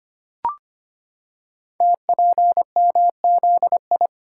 Morse code, heard as a keyed tone, about 25 wpm, 700 Hz. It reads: TPMZI
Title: Code Groups - 5 Letters - 25wpm